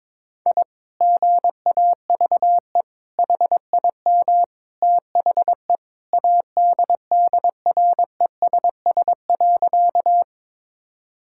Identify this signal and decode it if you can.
Morse code, heard as a keyed tone, about 22 wpm, 700 Hz. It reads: I GAVE HIM THE ADDRESS.
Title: Sentences from Top 1000 Words - Review Part 6 - 22wpm